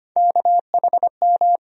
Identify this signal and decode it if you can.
Morse code, heard as a keyed tone, about 25 wpm, 700 Hz. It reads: KHM
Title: USA Call Sign Postfixes - 3 Letters 25wpm